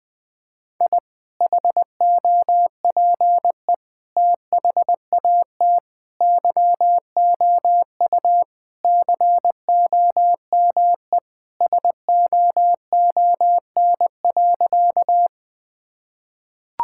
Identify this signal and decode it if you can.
Morse code, heard as a keyed tone, about 21 wpm, 700 Hz. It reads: I HOPE THAT YOU COME SOON.
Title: Sentences with 2-Word Phrases - 20wpm